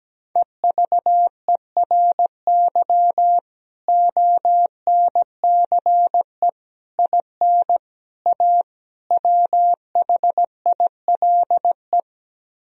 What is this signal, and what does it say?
Morse code, heard as a keyed tone, about 17 wpm, 700 Hz. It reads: EVERY ONCE IN A WHILE
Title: Speed-Racing - Sets of 5 Words 17wpm